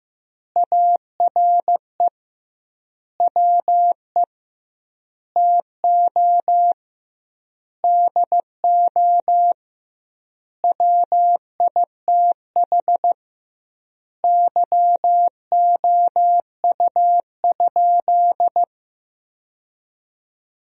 Morse code, heard as a keyed tone, about 16 wpm, 700 Hz. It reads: ARE WE TO DO WITH YOU?
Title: 2x Word-Spacing - Sentences from Top 100 Words - 15wpm